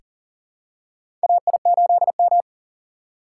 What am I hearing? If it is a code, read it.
Morse code, heard as a keyed tone, about 40 wpm, 700 Hz. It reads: AI8M
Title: USA Call Signs - 3x Character Spacing - 40wpm